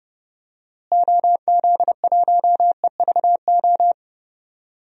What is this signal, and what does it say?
Morse code, heard as a keyed tone, about 30 wpm, 700 Hz. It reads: OZ1EVO
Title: DXCC-100 Call Signs - 6 Characters - Repeated 2x - 30wpm